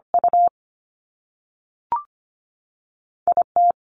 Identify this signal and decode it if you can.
Morse code, heard as a keyed tone, about 25 wpm, 700 Hz. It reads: U I T
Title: Top Two Letter Combinations in Words 25wpm